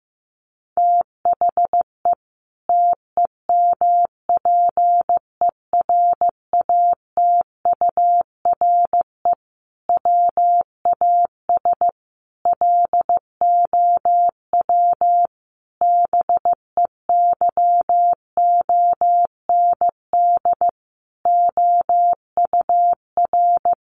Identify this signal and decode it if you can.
Morse code, heard as a keyed tone, about 15 wpm, 700 Hz. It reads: THE TEMPERATURE WAS LOW BEYOND OUR
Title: Sentences from Top 1500 Words 15wpm